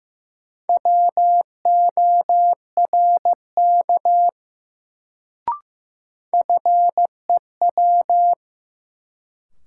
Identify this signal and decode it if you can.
Morse code, heard as a keyed tone, about 15 wpm, 700 Hz. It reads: WORK FEW
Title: Sets of 2 Words - New Word Drill - 15wpm